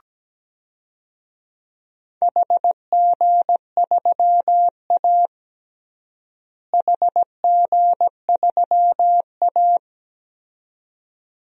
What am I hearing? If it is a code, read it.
Morse code, heard as a keyed tone, about 17 wpm, 700 Hz. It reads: HG3A HG3A
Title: DXCC-100 Call Signs - 4 Characters - Repeated 2x - 17wpm